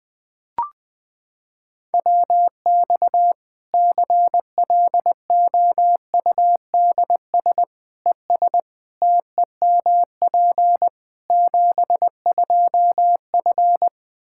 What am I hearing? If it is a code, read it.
Morse code, heard as a keyed tone, about 20 wpm, 700 Hz. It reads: WX CLOUDS ES TEMP 72F
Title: Practice QSO - Exchange 2 - Info - 20wpm